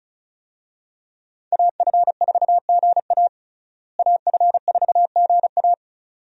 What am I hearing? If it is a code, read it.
Morse code, heard as a keyed tone, about 36 wpm, 700 Hz. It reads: AF4GA AF4GA
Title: USA Call Signs - 5 Characters - Repeated 2x - 35wpm